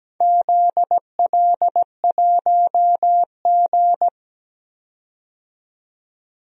Morse code, heard as a keyed tone, about 17 wpm, 700 Hz. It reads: ZL1G
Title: DXCC-100 Call Signs - 4 Characters - 17wpm